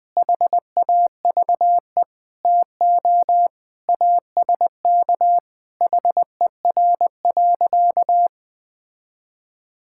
Morse code, heard as a keyed tone, about 20 wpm, 700 Hz. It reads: HAVE TO ASK HER.
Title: Sentences with 3-Word Phrases - 20wpm